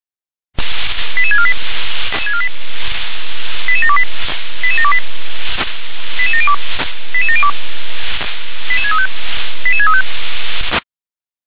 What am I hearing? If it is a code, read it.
Five-tone selective call (ZVEI), heard as quick runs of five tones, tones of about 80 ms. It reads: 88648 88728 88728 88702 88702 88635 88635